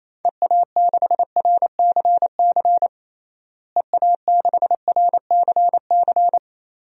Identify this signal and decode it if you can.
Morse code, heard as a keyed tone, about 28 wpm, 700 Hz. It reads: EA6RCC EA6RCC
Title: DXCC-100 Call Signs - 6 Characters - Repeated 3x - 28wpm